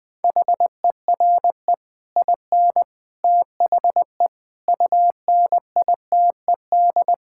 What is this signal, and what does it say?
Morse code, heard as a keyed tone, about 20 wpm, 700 Hz. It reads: HERE IN THE UNITED
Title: Sentences with 5-Word Phrases - 20wpm